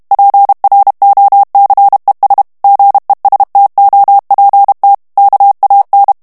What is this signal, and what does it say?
Morse code, heard as a keyed tone, about 32 wpm, 800 Hz. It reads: PROCES GESTOPT KAN
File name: HOLLANDAIS texte clair 2 vitesse 30 mots minute